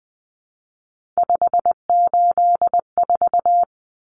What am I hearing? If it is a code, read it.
Morse code, heard as a keyed tone, about 20 wpm, 700 Hz. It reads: 584
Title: Numbers - 3-Digit 20wpm